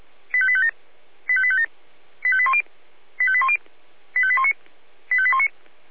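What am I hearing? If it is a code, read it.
Five-tone selective call (ZVEI), heard as quick runs of five tones, tones of about 70 ms. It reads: 85857 85857 85810 85810 85819 85819
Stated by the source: ZVEI five-tone call received on SDR.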